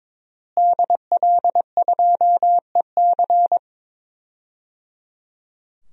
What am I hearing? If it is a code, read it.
Morse code, heard as a keyed tone, about 22 wpm, 700 Hz. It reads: DL2EC
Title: DXCC-100 Call Signs - 5 Characters - Repeated 3x - 22wpm